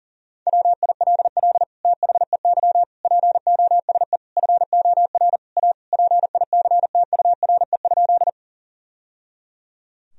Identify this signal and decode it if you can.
Morse code, heard as a keyed tone, about 40 wpm, 700 Hz. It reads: WILL THEY POSE FOR A PICTURE?
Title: Sentences from Top 1000 Words - Review Part 2 - 40wpm